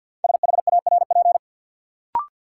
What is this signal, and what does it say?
Morse code, heard as a keyed tone, about 53 wpm, 700 Hz. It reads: SHARP
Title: Sentences from Top 1600 Words - New Word Drill - 50wpm